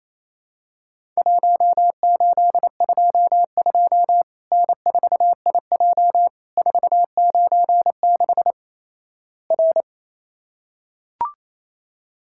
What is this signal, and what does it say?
Morse code, heard as a keyed tone, about 28 wpm, 700 Hz. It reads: 1822 N4IJ 496 R
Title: Worked All Europe DX Contest - QTC Exchange - 28wpm